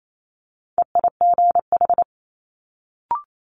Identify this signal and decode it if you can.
Morse code, heard as a keyed tone, about 28 wpm, 700 Hz. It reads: EIGH
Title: Multi-letter Phonemes Quick Fire 28wpm